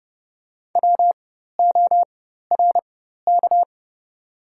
Morse code, heard as a keyed tone, about 30 wpm, spaced as for 14 wpm, 700 Hz. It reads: WORK
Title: 4x Character Spacing - Top 200 Words - 30wpm